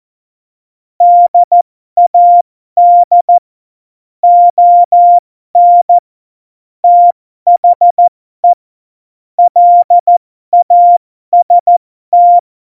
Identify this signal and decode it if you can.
Morse code, heard as a keyed tone, about 14 wpm, spaced as for 12 wpm, 700 Hz. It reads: DAD ON THE LAST
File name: Morsekurs / Witze